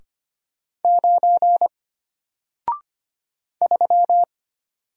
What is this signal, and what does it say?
Morse code, heard as a keyed tone, about 25 wpm, 700 Hz. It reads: 9 3
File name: Single-Letter-Number-25wpm